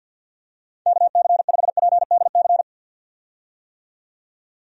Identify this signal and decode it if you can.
Morse code, heard as a keyed tone, about 50 wpm, 700 Hz. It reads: KC5PDC
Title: USA Call Signs - 6 Characters - 50wpm